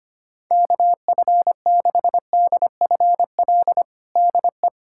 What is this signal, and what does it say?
Morse code, heard as a keyed tone, about 25 wpm, 700 Hz. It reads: KF6DFL DE
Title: USA Callsign DE Callsign - 6 Characters - 25wpm